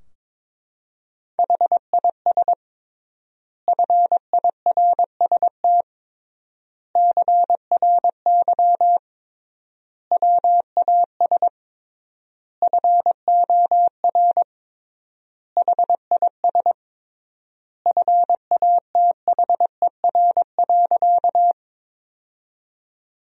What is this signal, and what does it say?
Morse code, heard as a keyed tone, about 22 wpm, 700 Hz. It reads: HIS FIRST CRY WAS FOR HIS FATHER.
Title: 3x Word-Spacing - Sentences from Top 500 Words - 22wpm